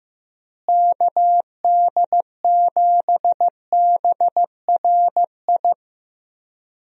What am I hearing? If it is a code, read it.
Morse code, heard as a keyed tone, about 15 wpm, 700 Hz. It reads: KD7BRI
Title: USA Call Signs - 6 Characters - Repeated 3x - 15wpm